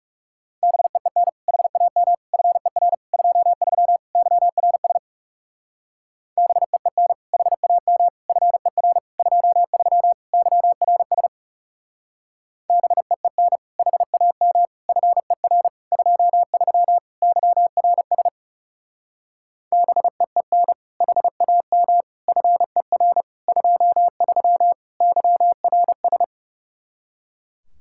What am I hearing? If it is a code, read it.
Morse code, heard as a keyed tone, about 38 wpm, 700 Hz. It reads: BEEN HAM FER 23 YRS BEEN HAM FER 23 YRS BEEN HAM FER 23 YRS BEEN HAM FER 23 YRS